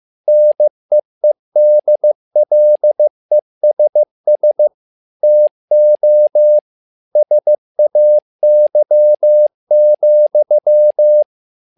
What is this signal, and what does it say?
Morse code, heard as a keyed tone, about 15 wpm, 600 Hz. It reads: NEEDLESS TO SAY,
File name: learning center / tower trouble / tower trouble 15wpm